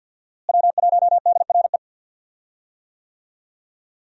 Morse code, heard as a keyed tone, about 53 wpm, 700 Hz. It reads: W1DRE
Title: USA Call Signs - 5 Characters - Repeated 2x - 50wpm